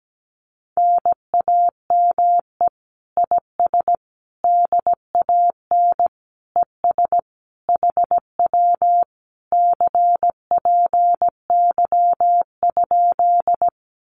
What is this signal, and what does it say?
Morse code, heard as a keyed tone, about 17 wpm, 700 Hz. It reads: NAME IS DAN ES HW CPY?
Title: Common QSO Phrases Encore 17wpm